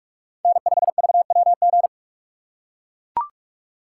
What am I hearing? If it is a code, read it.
Morse code, heard as a keyed tone, about 42 wpm, 700 Hz. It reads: N5VWG